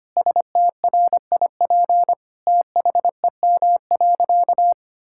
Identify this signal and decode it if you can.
Morse code, heard as a keyed tone, about 25 wpm, 700 Hz. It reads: STRIP THEM.
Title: Sentences from Top 1600 Words 25wpm